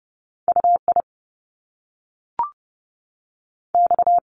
Morse code, heard as a keyed tone, about 30 wpm, 700 Hz. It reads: U I X
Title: Top Two Letter Combinations in Words 30wpm